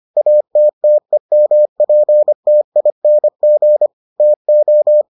Morse code, heard as a keyed tone, about 25 wpm, 600 Hz. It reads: ATTEMPTING TO